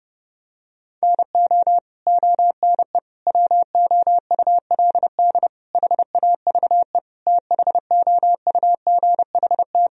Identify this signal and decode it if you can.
Morse code, heard as a keyed tone, about 29 wpm, 700 Hz. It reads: NO ONE WOULD HAVE THOUGHT
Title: Sentences from Top 1000 Words - Review Part 1 - 30wpm